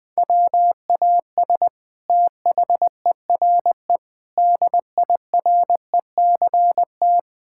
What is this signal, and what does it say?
Morse code, heard as a keyed tone, about 21 wpm, 700 Hz. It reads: WAS THERE DIRECT
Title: Sentences from Top 1000 Words - Review Part 6 - 20wpm